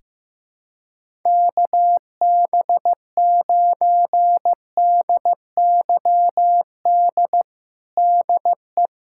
Morse code, heard as a keyed tone, about 15 wpm, 700 Hz. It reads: KB9DYD DE
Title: USA Callsign DE Callsign - 6 Characters - 15wpm